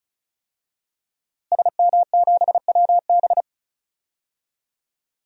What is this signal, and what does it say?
Morse code, heard as a keyed tone, about 34 wpm, 700 Hz. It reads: SM7WB
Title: DXCC-100 Call Signs - 5 Characters - Repeated 2x - 35wpm